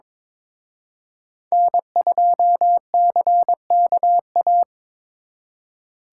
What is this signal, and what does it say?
Morse code, heard as a keyed tone, about 26 wpm, 700 Hz. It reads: N2CKA
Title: Speed Racing - USA Call Signs 22wpm